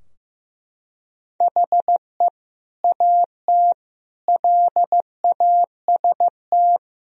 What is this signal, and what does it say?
Morse code, heard as a keyed tone, about 16 wpm, 700 Hz. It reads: HE AT LAST
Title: Sentences from Top 1000 Words - Review Part 2 - 15wpm